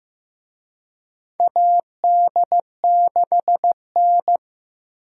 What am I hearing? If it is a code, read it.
Morse code, heard as a keyed tone, about 15 wpm, 700 Hz. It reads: AD6N
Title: USA Call Signs - 4 Characters - 15wpm